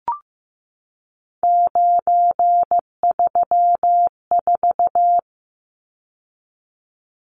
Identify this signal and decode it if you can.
Morse code, heard as a keyed tone, about 15 wpm, 700 Hz. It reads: 934
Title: Numbers - 3-Digit 15wpm